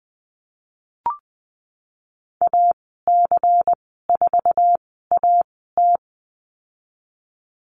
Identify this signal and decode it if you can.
Morse code, heard as a keyed tone, about 19 wpm, spaced as for 14 wpm, 700 Hz. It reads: AC4AT
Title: 193 Lesson 31g - Letter v - Call Signs 20wpm